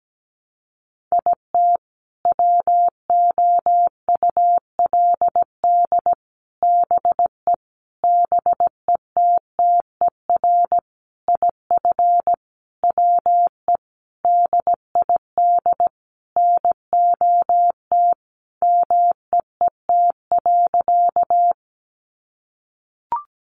Morse code, heard as a keyed tone, about 17 wpm, 700 Hz. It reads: IT WOULD BE BETTER IF WE DID NOT MEET.
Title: Sentences from Top 1100 Words 17wpm